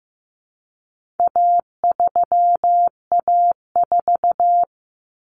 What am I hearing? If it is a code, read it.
Morse code, heard as a keyed tone, about 15 wpm, 700 Hz. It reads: A3A4
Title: Common Cut Numbers - 4-Digit 15wpm